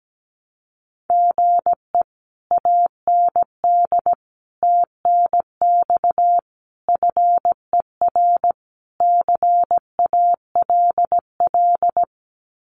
Morse code, heard as a keyed tone, about 17 wpm, 700 Hz. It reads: GE AND TNX FER CALL
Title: Common QSO Phrases Encore 17wpm